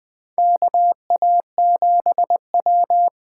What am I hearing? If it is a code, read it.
Morse code, heard as a keyed tone, about 20 wpm, 700 Hz. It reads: KA7W
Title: USA Call Signs - 6 Characters - 8x Character Spacing - 20wpm